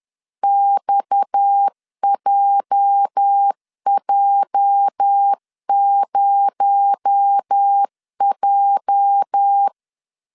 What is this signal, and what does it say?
Morse code, heard as a keyed tone, about 10 wpm, 800 Hz. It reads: XJJ0J
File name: LEZIONI / 8 YJ45 / 8 YJ 10wpm 2651